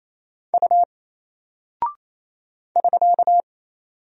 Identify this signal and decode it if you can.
Morse code, heard as a keyed tone, about 27 wpm, 700 Hz. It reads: U <SK>
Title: Single Character (Letter:Number:Punctuation:Prosign) Alt 28wpm